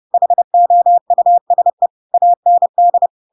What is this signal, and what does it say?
Morse code, heard as a keyed tone, about 30 wpm, 700 Hz. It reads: HOUSE AND
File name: Pinocchio / Pinocchio ch7 30 30 ik1hge